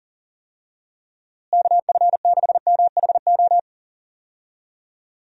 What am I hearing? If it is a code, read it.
Morse code, heard as a keyed tone, about 40 wpm, 700 Hz. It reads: KF6MHO
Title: USA Call Signs - 6 Characters - Repeated 2x - 40wpm